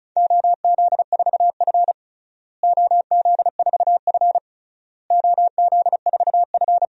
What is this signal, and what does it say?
Morse code, heard as a keyed tone, about 36 wpm, 700 Hz. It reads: OZ4F OZ4F OZ4F
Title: DXCC-100 Call Signs - 4 Characters - Repeated 3x - 35wpm